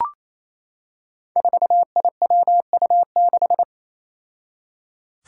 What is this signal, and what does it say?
Morse code, heard as a keyed tone, about 29 wpm, 700 Hz. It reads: 4IWU6
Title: Five Challenging Letter-Number Combinations 28wpm